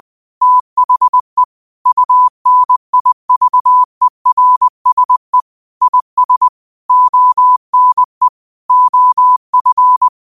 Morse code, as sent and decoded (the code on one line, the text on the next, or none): - .... / ..- -. .. ...- . .-. ... / .. ... / --- -. / --- ..-.
THE UNIVERSE IS ONE OF